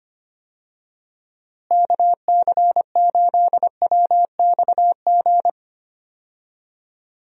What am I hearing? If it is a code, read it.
Morse code, heard as a keyed tone, about 25 wpm, 700 Hz. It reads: KC8WXG
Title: USA Call Signs - 6 Characters - Repeated 2x - 25wpm